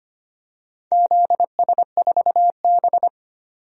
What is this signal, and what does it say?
Morse code, heard as a keyed tone, about 25 wpm, 700 Hz. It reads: ZS4B
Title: DXCC-100 Call Signs - 4 Characters - 25wpm